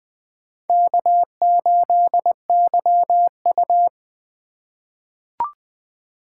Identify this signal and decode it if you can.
Morse code, heard as a keyed tone, about 20 wpm, 700 Hz. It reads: K8YU
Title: USA Call Signs - 4 Characters - 4x Character Spacing - 20wpm